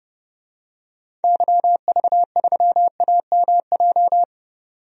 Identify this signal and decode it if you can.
Morse code, heard as a keyed tone, about 31 wpm, 700 Hz. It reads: YV3AMJ
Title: DXCC-100 Call Signs - 6 Characters - 30wpm